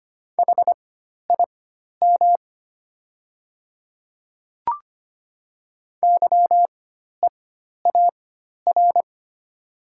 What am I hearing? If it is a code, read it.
Morse code, heard as a keyed tone, about 25 wpm, spaced as for 12 wpm, 700 Hz. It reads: HIM YEAR
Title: 4x Character Spacing - Top 200 Words - 25wpm